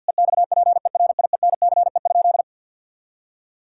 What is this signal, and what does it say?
Morse code, heard as a keyed tone, about 50 wpm, 700 Hz. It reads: EXPERIENCE?
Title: Sentences from Top 1000 Words - Review Part 5 - 50wpm